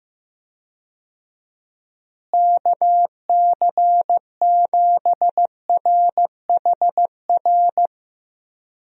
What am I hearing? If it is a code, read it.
Morse code, heard as a keyed tone, about 15 wpm, 700 Hz. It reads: KC7RHR
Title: USA Call Signs - 6 Characters - Repeated 3x - 15wpm